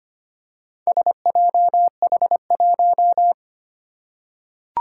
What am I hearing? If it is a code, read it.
Morse code, heard as a keyed tone, about 25 wpm, 700 Hz. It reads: SJH1
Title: Four Challenging Letter-Number Combinations 25wpm